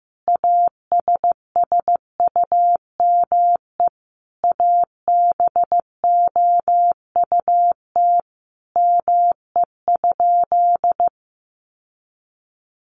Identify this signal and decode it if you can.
Morse code, heard as a keyed tone, about 15 wpm, 700 Hz. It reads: ASSUME ABOUT ME?
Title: Sentences from Top 1500 Words 15wpm